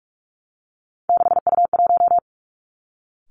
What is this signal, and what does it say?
Morse code, heard as a keyed tone, about 46 wpm, 700 Hz. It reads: 6U1
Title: Three Challenging Letter-Number Combinations 45wpm